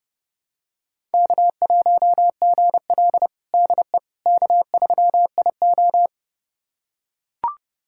Morse code, heard as a keyed tone, about 30 wpm, 700 Hz. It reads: K1GL DE K3IO
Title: USA Callsign DE Callsign - 4 Characters - 30wpm